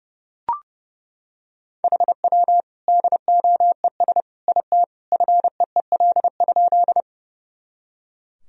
Morse code, heard as a keyed tone, about 30 wpm, 700 Hz. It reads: HW DOES IT FEEL?
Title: Sentences from Top 400 Words and Abbreviations 30wpm